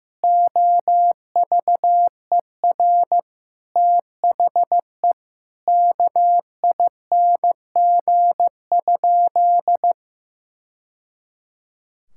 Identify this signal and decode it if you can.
Morse code, heard as a keyed tone, about 15 wpm, 700 Hz. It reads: OVER THE KING?
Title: Sentences from Top 1000 Words - Review Part 6 - 15wpm